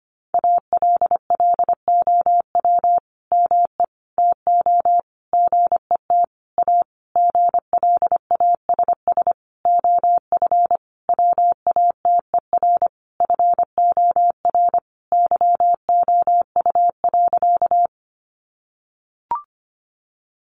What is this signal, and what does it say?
Morse code, heard as a keyed tone, about 25 wpm, 700 Hz. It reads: ALLOW ME TO GET A GLASS OF WATER FOR YOU.
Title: Sentences from Top 1300 Words 25wpm